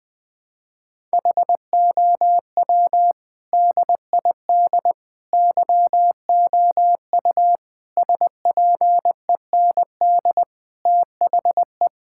Morse code, heard as a keyed tone, about 20 wpm, 700 Hz. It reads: HOW DID YOU SPEND THE
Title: Sentences from Top 1000 Words - Review Part 5 - 20wpm